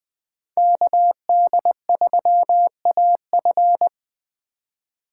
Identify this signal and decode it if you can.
Morse code, heard as a keyed tone, about 21 wpm, 700 Hz. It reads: KD3AF